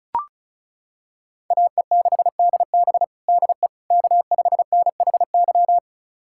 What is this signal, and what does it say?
Morse code, heard as a keyed tone, about 34 wpm, 700 Hz. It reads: AE6DB DE K5NHY
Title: USA Callsign DE Callsign - 5 Characters - 35wpm